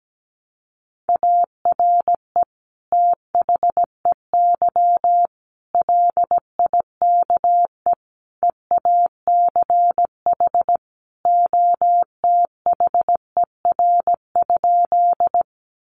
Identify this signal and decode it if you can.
Morse code, heard as a keyed tone, about 17 wpm, 700 Hz. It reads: ARE THEY LIKE EACH OTHER?
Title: Sentences from Top 200 Words 17wpm